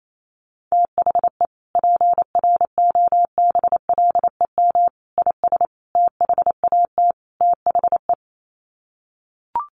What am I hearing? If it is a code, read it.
Morse code, heard as a keyed tone, about 28 wpm, 700 Hz. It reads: THE PROBLEM IS THAT THE
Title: Sets of 5 Words Encore2 28wpm